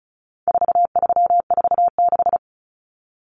Morse code, heard as a keyed tone, about 34 wpm, 700 Hz. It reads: 4346
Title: Numbers - 4-Digit 35wpm